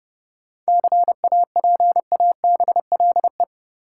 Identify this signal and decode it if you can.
Morse code, heard as a keyed tone, about 30 wpm, 700 Hz. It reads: CAPABLE
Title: Sentences from Top 1200 Words - New Word Drill Alt - 30wpm